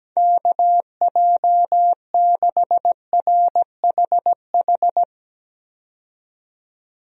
Morse code, heard as a keyed tone, about 17 wpm, 700 Hz. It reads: KJ6RHH
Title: USA Call Signs - 6 Characters - 17wpm